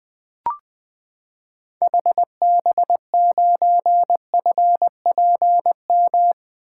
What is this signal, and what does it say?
Morse code, heard as a keyed tone, about 20 wpm, 700 Hz. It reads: HB9FPM
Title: 2023 SOTA Activator Call Signs - 20wpm